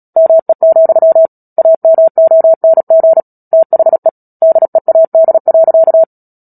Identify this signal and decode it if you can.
Morse code, heard as a keyed tone, about 37 wpm, 650 Hz. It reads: ME, AMONG THE DEAD.
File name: Book Two / 10-Chapter 10 - Epilogue - 36 WPM